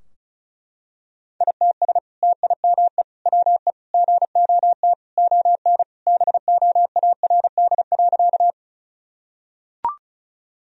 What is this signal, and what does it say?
Morse code, heard as a keyed tone, about 34 wpm, 700 Hz. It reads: ITS TIME WE GOT ON BOARD.
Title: Sentences from Top 1000 Words - Review Part 4 - 35wpm